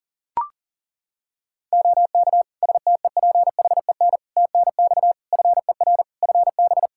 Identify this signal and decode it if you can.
Morse code, heard as a keyed tone, about 40 wpm, 700 Hz. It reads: OK STEPHEN TNX FER FB
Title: K1USN SST QSO Practice 40wpm